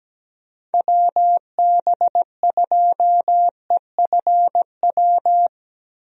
Morse code, heard as a keyed tone, about 17 wpm, 700 Hz. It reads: WB2EFW
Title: USA Call Signs - 6 Characters - Repeated 3x - 17wpm